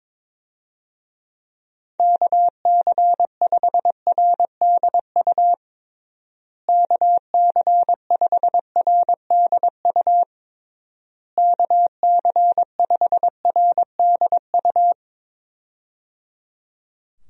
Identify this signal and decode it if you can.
Morse code, heard as a keyed tone, about 22 wpm, 700 Hz. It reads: KC5RDU KC5RDU KC5RDU
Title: USA Call Signs - 6 Characters - Repeated 3x - 22wpm